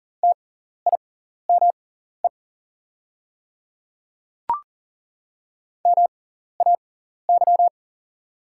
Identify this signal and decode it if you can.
Morse code, heard as a keyed tone, about 40 wpm, spaced as for 14 wpm, 690 Hz. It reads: TIME MAY
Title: 6x Character Spacing - Top 200 Words - 40wpm